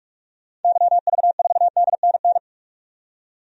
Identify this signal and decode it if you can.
Morse code, heard as a keyed tone, about 44 wpm, 700 Hz. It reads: YV4DNN
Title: DXCC-100 Call Signs - 6 Characters - 45wpm